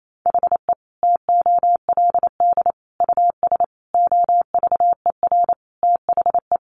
Morse code, heard as a keyed tone, about 28 wpm, 700 Hz. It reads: HE TOLD US OVER THE
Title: Sentences from Top 1100 Words 28wpm